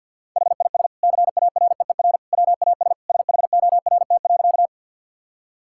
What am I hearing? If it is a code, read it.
Morse code, heard as a keyed tone, about 50 wpm, 700 Hz. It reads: HIS CAREER WAS SHORT.
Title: Sentences from Top 1000 Words - Review Part 6 - 50wpm